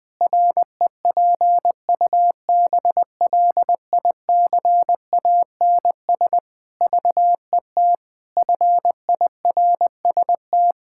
Morse code, heard as a keyed tone, about 21 wpm, 700 Hz. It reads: REPUBLICANS VET FIRST
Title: Pangram Challenge v2 - 20wpm